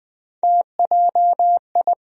Morse code, heard as a keyed tone, about 20 wpm, 700 Hz. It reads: TJI
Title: USA Call Sign Postfixes - 3 Letters 20wpm